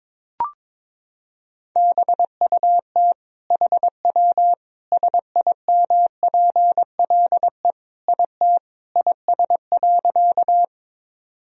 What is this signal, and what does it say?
Morse code, heard as a keyed tone, about 22 wpm, 700 Hz. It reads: BUT HW SIMPLE IT IS.
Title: Sentences from Top 400 Words and Abbreviations 22wpm